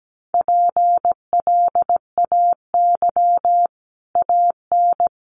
Morse code, heard as a keyed tone, about 17 wpm, 700 Hz. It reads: PLAY AN
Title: Sets of 5 Words Encore 17wpm